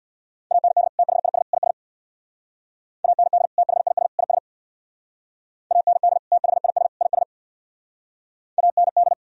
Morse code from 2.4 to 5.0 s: AND THIS IS

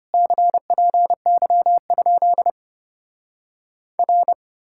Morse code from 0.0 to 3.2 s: CPY?

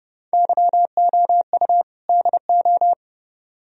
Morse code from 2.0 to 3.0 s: DO

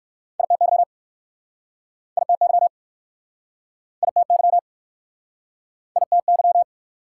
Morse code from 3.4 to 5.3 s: ITY

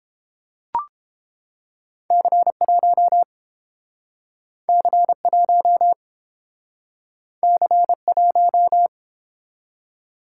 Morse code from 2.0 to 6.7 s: C1 C1

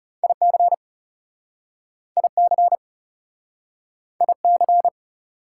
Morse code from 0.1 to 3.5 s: IC IC

